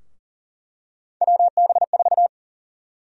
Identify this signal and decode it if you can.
Morse code, heard as a keyed tone, about 40 wpm, 700 Hz. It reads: WB4
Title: USA Call Sign Prefixes - Repeated 2x - 40wpm